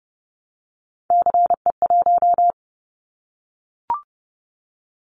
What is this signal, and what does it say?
Morse code, heard as a keyed tone, about 30 wpm, 700 Hz. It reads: CE1
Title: Callsign Prefixes for Top 100 DXCC - 30wpm